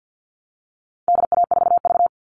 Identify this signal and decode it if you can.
Morse code, heard as a keyed tone, about 50 wpm, 700 Hz. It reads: DA4V
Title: All Cut Numbers - 4-Digit 50wpm